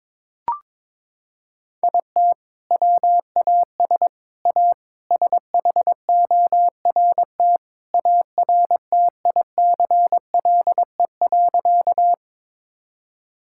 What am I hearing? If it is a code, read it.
Morse code, heard as a keyed tone, about 23 wpm, 700 Hz. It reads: IT WAS A SHORT ARTICLE.
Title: Sentences from Top 1000 Words - Review Part 5 - 22wpm